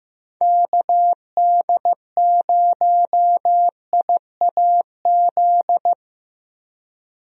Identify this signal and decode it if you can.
Morse code, heard as a keyed tone, about 15 wpm, 700 Hz. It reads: KD0IAZ
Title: USA Call Signs - 6 Characters - Repeated 2x - 15wpm